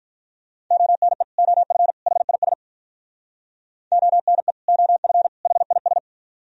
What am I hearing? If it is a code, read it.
Morse code, heard as a keyed tone, about 46 wpm, 700 Hz. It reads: ONE OF HIS ONE OF HIS